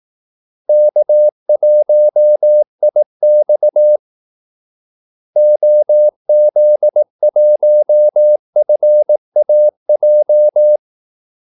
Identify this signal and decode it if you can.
Morse code, heard as a keyed tone, about 18 wpm, 600 Hz. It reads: K1IX OZ1FAJ
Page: Kaldesignaler 18 wpm